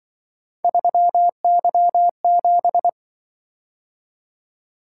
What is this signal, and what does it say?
Morse code, heard as a keyed tone, about 24 wpm, 700 Hz. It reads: 3Y7
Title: Speed-Racing - Three Letter-Number Combinations - 20wpm